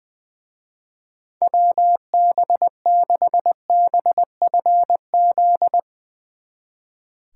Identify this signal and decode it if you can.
Morse code, heard as a keyed tone, about 20 wpm, 700 Hz. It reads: WB6BFZ
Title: Speed Racing - USA 2x3 Call Signs 20wpm